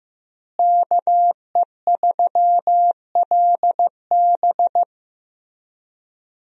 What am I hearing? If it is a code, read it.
Morse code, heard as a keyed tone, about 15 wpm, 700 Hz. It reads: KE3LB
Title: USA Call Signs - 5 Characters - Repeated 3x - 15wpm